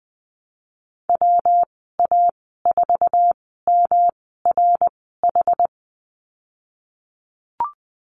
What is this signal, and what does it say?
Morse code, heard as a keyed tone, about 20 wpm, spaced as for 14 wpm, 700 Hz. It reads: WA4MRH
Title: 122 Lesson 20g - Number 3 - Call Signs 20wpm